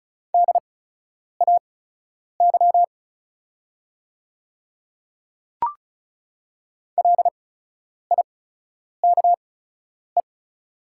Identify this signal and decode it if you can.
Morse code, heard as a keyed tone, about 36 wpm, spaced as for 10 wpm, 690 Hz. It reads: DAY LIKE
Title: 8x Character Spacing - Top 200 Words - 35wpm